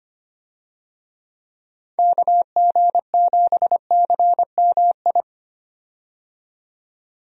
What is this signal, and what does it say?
Morse code, heard as a keyed tone, about 25 wpm, 700 Hz. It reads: KG7CMI
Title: USA Call Signs - 6 Characters - 25wpm